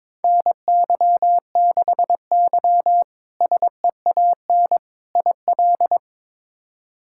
Morse code, heard as a keyed tone, about 22 wpm, 700 Hz. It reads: NY6Y SEAN IL
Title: K1USN SST QSO Practice - 22wpm